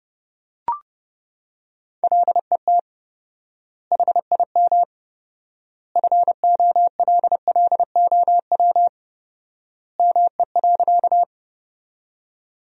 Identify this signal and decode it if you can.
Morse code, heard as a keyed tone, about 32 wpm, 700 Hz. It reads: LET HIM FOLLOW ME.
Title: 4x Word-Spacing - Sentences from Top 300 Words - 30wpm